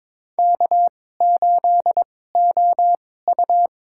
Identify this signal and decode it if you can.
Morse code, heard as a keyed tone, about 22 wpm, spaced as for 16 wpm, 700 Hz. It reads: K8OU
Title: USA Call Signs - 4 Characters - 2x Character Spacing - 22wpm